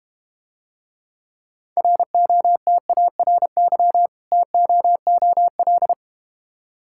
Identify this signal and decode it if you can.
Morse code, heard as a keyed tone, about 32 wpm, 700 Hz. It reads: ROTARY TOOL